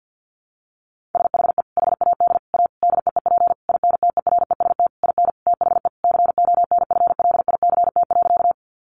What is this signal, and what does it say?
Morse code, heard as a keyed tone, about 50 wpm, 700 Hz. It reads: SHE HAD A DEEP INTEREST IN THE CONFLICT.
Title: Sentences from Top 1300 Words 50wpm